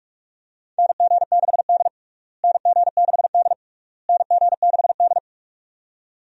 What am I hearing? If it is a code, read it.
Morse code, heard as a keyed tone, about 41 wpm, 700 Hz. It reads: NG6D NG6D NG6D
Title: USA Call Signs - 4 Characters - Repeated 3x - 45wpm